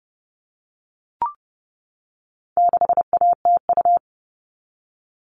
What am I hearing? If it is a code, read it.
Morse code, heard as a keyed tone, about 30 wpm, 700 Hz. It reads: 6ATU
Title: All Cut Numbers - 4-Digit 30wpm